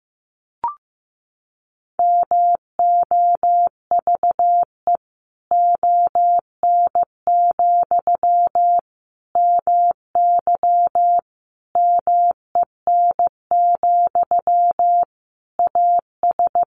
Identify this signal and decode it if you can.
Morse code, heard as a keyed tone, about 15 wpm, 700 Hz. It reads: MOVE ON, MY MEN, AS
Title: Sentences from Top 500 Words 15wpm